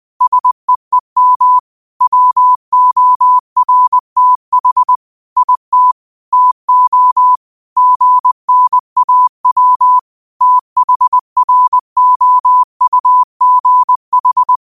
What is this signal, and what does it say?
Morse code, heard as a keyed tone, about 20 wpm, 1000 Hz. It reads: SEEM WORTH IT TO GNAW THROUGH
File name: qotdfiles / 20wpm